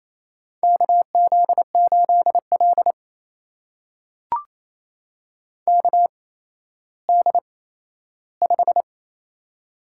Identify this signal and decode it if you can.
Morse code, heard as a keyed tone, about 28 wpm, spaced as for 14 wpm, 700 Hz. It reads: KZ8L KD5
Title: USA Call Signs - 8x Character Spacing - 28wpm